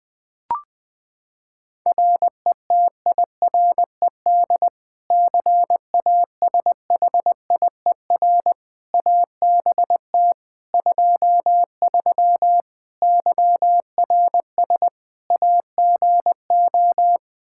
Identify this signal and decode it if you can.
Morse code, heard as a keyed tone, about 20 wpm, 700 Hz. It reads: RETIRED CASHIER ABT 23 YRS AGO
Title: Practice QSO - Exchange 3 - More Info - 20wpm